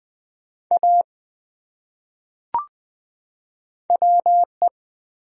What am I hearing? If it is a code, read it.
Morse code, heard as a keyed tone, about 20 wpm, 700 Hz. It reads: A WE
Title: Top 100 Words 20wpm